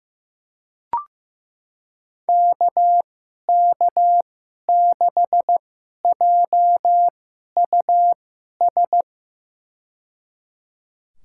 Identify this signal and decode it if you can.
Morse code, heard as a keyed tone, about 15 wpm, spaced as for 11 wpm, 700 Hz. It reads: KK6JUS
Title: USA Call Signs - 6 Characters - 2x Character Spacing - 15wpm